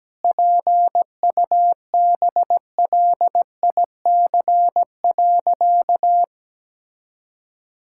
Morse code, heard as a keyed tone, about 17 wpm, 700 Hz. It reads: PUBLIC.
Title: Sentences from Top 1000 Words - Review Part 5 - 17wpm